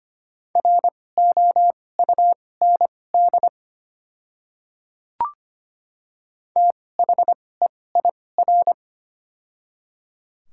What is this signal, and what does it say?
Morse code, heard as a keyed tone, about 25 wpm, spaced as for 18 wpm, 700 Hz. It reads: ROUND THEIR
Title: 2x Character Spacing - Top 300 Words - 25wpm